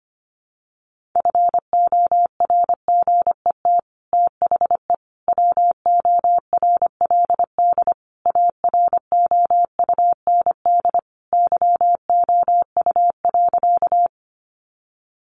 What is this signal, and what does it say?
Morse code, heard as a keyed tone, about 25 wpm, 700 Hz. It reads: FORGET THE WORLD AROUND YOU.
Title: Sentences from Top 1200 Words 25wpm